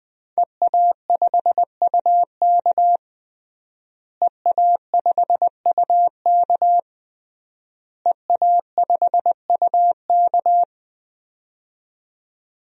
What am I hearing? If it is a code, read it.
Morse code, heard as a keyed tone, about 20 wpm, 700 Hz. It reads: EA5UK EA5UK EA5UK
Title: DXCC-100 Call Signs - 5 Characters - Repeated 3x - 20wpm